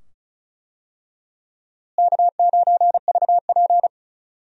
Morse code, heard as a keyed tone, about 36 wpm, 700 Hz. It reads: K9VP
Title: USA Call Signs - 4 Characters - Repeated 2x - 35wpm